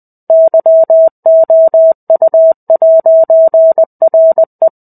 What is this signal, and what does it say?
Morse code, heard as a keyed tone, about 20 wpm, 650 Hz. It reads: YOU'RE
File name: 05-Chapter-05 20 WPM